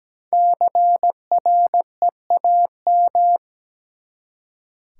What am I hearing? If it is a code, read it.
Morse code, heard as a keyed tone, about 17 wpm, 700 Hz. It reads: CREAM
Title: Five Letter Words - 17wpm